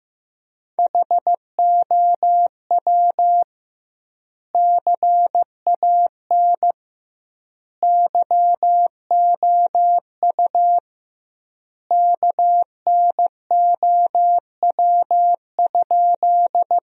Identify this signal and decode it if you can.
Morse code, heard as a keyed tone, about 15 wpm, 700 Hz. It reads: HOW CAN YOU KNOW?
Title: 2x Word-Spacing - Sentences from Top 100 Words - 15wpm